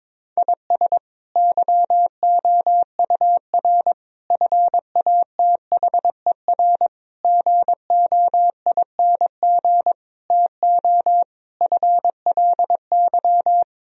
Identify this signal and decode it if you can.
Morse code, heard as a keyed tone, about 22 wpm, 700 Hz. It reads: IS YOUR FATHER GOING TO FLY
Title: Sentences from Top 1000 Words - Review Part 2 - 22wpm